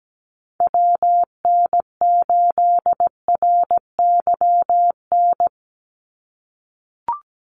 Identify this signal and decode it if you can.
Morse code, heard as a keyed tone, about 17 wpm, 700 Hz. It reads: WN8RYN
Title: USA Call Signs 17wpm